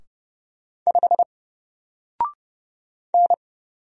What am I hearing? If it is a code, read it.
Morse code, heard as a keyed tone, about 30 wpm, 700 Hz. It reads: 5 N
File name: Single-Letter-Number-30wpm